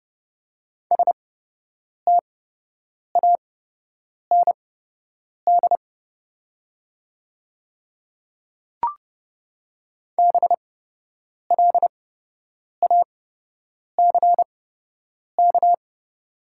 Morse code, heard as a keyed tone, about 30 wpm, spaced as for 8 wpm, 700 Hz. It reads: STAND BLACK
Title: 8x Character Spacing - Top 500 Words - 30wpm